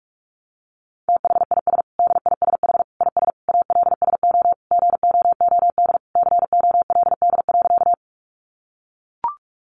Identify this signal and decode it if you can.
Morse code, heard as a keyed tone, about 44 wpm, 700 Hz. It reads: THIS DISH IS ALSO GOOD COLD.
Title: Sentences from Top 1300 Words 45wpm